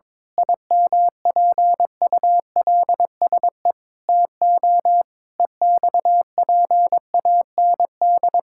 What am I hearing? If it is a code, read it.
Morse code, heard as a keyed tone, about 22 wpm, 700 Hz. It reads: IMPULSE TO EXPAND
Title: Pangram Challenge v2 - 22wpm